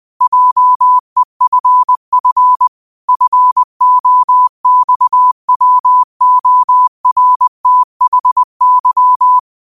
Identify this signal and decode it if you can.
Morse code, heard as a keyed tone, about 20 wpm, 1000 Hz. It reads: JEFF FOXWORTHY
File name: qotdfiles / 20wpm